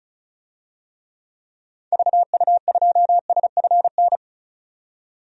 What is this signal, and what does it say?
Morse code, heard as a keyed tone, about 36 wpm, 700 Hz. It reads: VU2SFN
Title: DXCC-100 Call Signs - 6 Characters - 35wpm